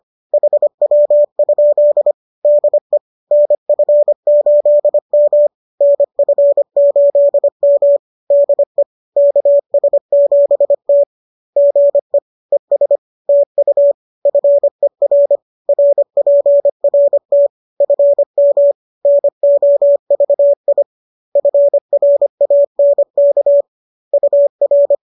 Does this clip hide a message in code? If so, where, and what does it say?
Yes: Morse code from 0.3 to 24.9 s, heard as a keyed tone, about 25 wpm, 570 Hz. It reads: HW? DE NF8M NF8M DE KS7T GE ES TU FER RPRT FM NOVI FRANK UR